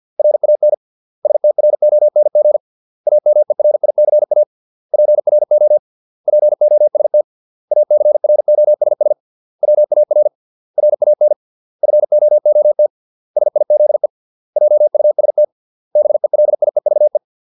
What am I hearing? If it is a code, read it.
Morse code, heard as a keyed tone, about 50 wpm, 600 Hz. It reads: RAN STRONG AMERICA PRO POST ACROSS WAR RAN FOOT SIZE JUST BELIEVE